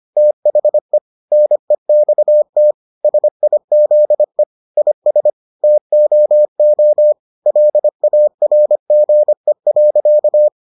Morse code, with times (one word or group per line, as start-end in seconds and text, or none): 0.2-1.0 THE
1.3-2.7 NEXT
3.0-4.4 SIZE
4.8-5.3 IS
5.6-7.1 TOO
7.5-10.5 LARGE.